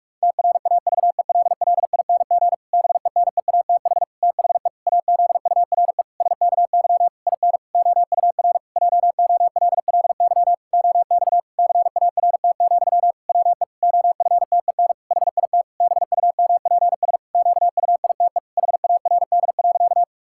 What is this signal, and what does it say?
Morse code, heard as a keyed tone, about 46 wpm, 700 Hz. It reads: TRAVELLING BENEATH THE AZURE SKY IN OUR JOLLY OX CART, WE OFTEN HIT BUMPS QUITE HARD.
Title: Pangram Challenge v2 - 45wpm